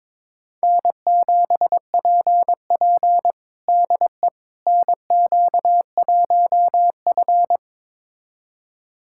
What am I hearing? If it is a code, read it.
Morse code, heard as a keyed tone, about 22 wpm, 700 Hz. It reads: N7PP DE NQ1F
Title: USA Callsign DE Callsign - 4 Characters - 22wpm